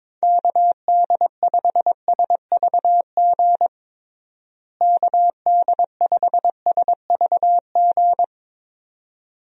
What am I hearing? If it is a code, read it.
Morse code, heard as a keyed tone, about 22 wpm, 700 Hz. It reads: KD5SVG KD5SVG
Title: USA Call Signs - 6 Characters - Repeated 2x - 22wpm